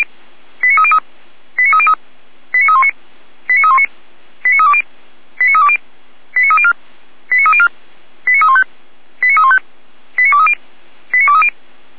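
Five-tone selective call (ZVEI), heard as quick runs of five tones, tones of about 70 ms. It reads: 89303 89303 89319 89319 89320 89320 89304 89304 89315 89315 89230 89230
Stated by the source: Five-tone ZVEI selective call 89125, received on an SDR.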